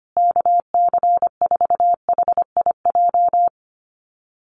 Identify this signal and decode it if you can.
Morse code, heard as a keyed tone, about 25 wpm, 700 Hz. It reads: KC4HIJ
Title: USA Call Signs Repeated 2x 25wpm